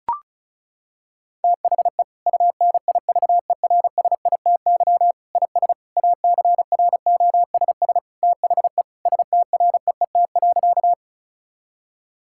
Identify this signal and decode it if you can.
Morse code, heard as a keyed tone, about 36 wpm, 700 Hz. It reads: THE UNIVERSITY IS ACROSS THE STREET.
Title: Sentences from Top 1000 Words - Review Part 5 - 35wpm